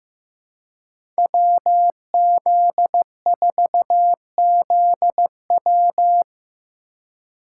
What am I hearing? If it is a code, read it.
Morse code, heard as a keyed tone, about 15 wpm, 700 Hz. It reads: WZ4ZW
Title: USA Call Signs - 5 Characters - 2x Character Spacing - 15wpm